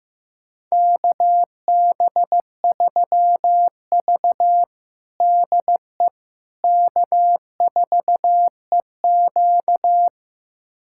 Morse code, heard as a keyed tone, about 15 wpm, 700 Hz. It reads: KB3V DE K4EQ
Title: USA Callsign DE Callsign - 4 Characters - 15wpm